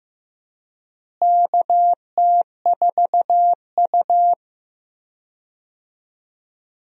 Morse code, heard as a keyed tone, about 15 wpm, 700 Hz. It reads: KT4U
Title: USA Call Signs - 4 Characters - Repeated 3x - 15wpm